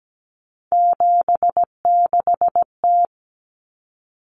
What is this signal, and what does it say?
Morse code, heard as a keyed tone, about 17 wpm, 690 Hz. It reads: 76T
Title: Common Cut Numbers - 3-Digit 17wpm